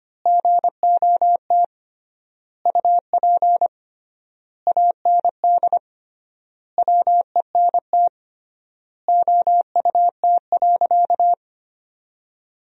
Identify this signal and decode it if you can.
Morse code, heard as a keyed tone, about 25 wpm, 700 Hz. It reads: GOT UP AND WENT OUT.
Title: 3x Word-Spacing - Sentences from Top 300 Words - 25wpm